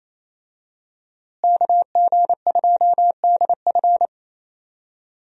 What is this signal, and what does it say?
Morse code, heard as a keyed tone, about 28 wpm, 700 Hz. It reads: KG2DF
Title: USA Call Signs - 5 Characters - Repeated 3x - 28wpm